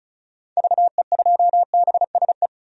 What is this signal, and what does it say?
Morse code, heard as a keyed tone, about 36 wpm, 700 Hz. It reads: VE2BSE
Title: DXCC-100 Call Signs - 6 Characters - 35wpm